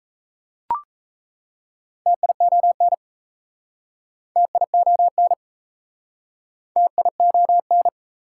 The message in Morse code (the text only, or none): TION TION TION